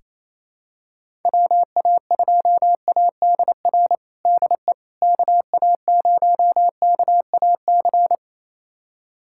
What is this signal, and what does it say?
Morse code, heard as a keyed tone, about 28 wpm, 700 Hz. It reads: WA2ADR DE KA0KAC
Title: USA Callsign DE Callsign - 6 Characters - 28wpm